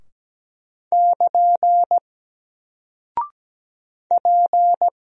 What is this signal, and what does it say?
Morse code, heard as a keyed tone, about 17 wpm, 700 Hz. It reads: <KN> P
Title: Single Character (Letter:Number:Punctuation:Prosign) 17wpm